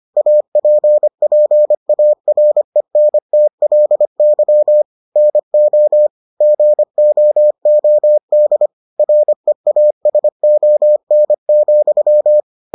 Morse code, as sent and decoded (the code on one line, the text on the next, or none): .- .--. .--. .- .-. -. - .-.. -.-- / -. --- / --. --- --- -.. / .-. .- ... --- -. --..--
APPARENTLY NO GOOD REASON,